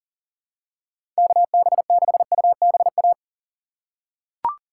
Morse code, heard as a keyed tone, about 40 wpm, 700 Hz. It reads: KB6UBA
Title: USA Call Signs - 6x Character Spacing - 40wpm